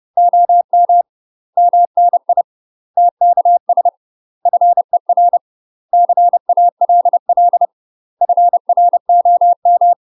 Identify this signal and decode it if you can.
Morse code, heard as a keyed tone, about 30 wpm, 700 Hz. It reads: OM MNI TKS FER CALL FROM
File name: UK MASTER QSOs TRAINING / 30WPM / UK MASTER QSOs 0028